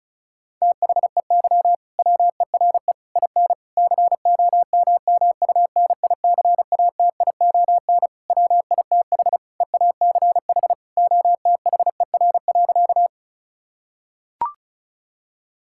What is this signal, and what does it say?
Morse code, heard as a keyed tone, about 34 wpm, 700 Hz. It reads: THEY WERE IN COMMUNICATION WITH EACH OTHER.
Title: Sentences from Top 1000 Words - Review Part 6 - 35wpm